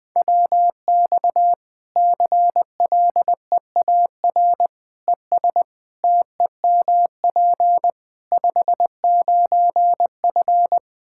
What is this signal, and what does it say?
Morse code, heard as a keyed tone, about 21 wpm, 700 Hz. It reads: WX CLEAR ES TEMP 59F
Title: Practice QSO - Exchange 2 - Info - 20wpm